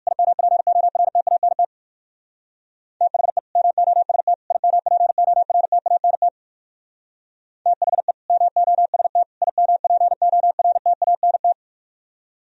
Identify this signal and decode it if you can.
Morse code, heard as a keyed tone, about 58 wpm, 700 Hz. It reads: IMPORTANT THE MOST IMPORTANT THE MOST IMPORTANT